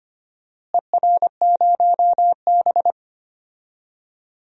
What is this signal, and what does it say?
Morse code, heard as a keyed tone, about 25 wpm, 700 Hz. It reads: ER0B
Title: DXCC-100 Call Signs - 4 Characters - Repeated 2x - 25wpm